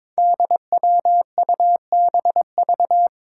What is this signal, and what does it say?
Morse code, heard as a keyed tone, about 23 wpm, 700 Hz. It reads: DWUBV
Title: Five Challenging Letter-Number Combinations 22wpm